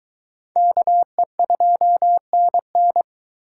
Morse code, heard as a keyed tone, about 22 wpm, 700 Hz. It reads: KE2NN